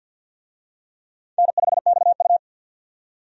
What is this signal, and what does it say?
Morse code, heard as a keyed tone, about 51 wpm, 700 Hz. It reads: N5XU
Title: USA Call Signs - 4 Characters - 50wpm